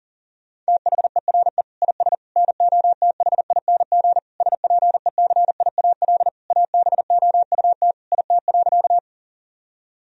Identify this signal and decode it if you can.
Morse code, heard as a keyed tone, about 40 wpm, 700 Hz. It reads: THERE IS NOTHING SPECIAL ABOUT IT.
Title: Sentences from Top 1000 Words - Review Part 3 - 40wpm